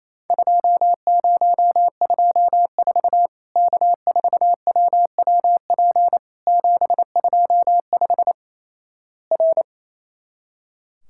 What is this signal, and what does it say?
Morse code, heard as a keyed tone, about 28 wpm, 700 Hz. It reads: 2024 K4WWP 725 R
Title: Worked All Europe DX Contest - QTC Exchange - 28wpm